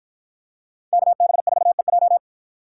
Morse code, heard as a keyed tone, about 53 wpm, 700 Hz. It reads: KB4EJ